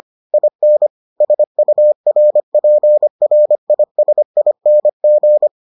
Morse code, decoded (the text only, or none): IN SURPRISING